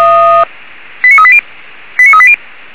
Five-tone selective call (ZVEI), heard as quick runs of five tones, tones of about 70 ms. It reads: 89380 89380
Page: SDR/ZVEI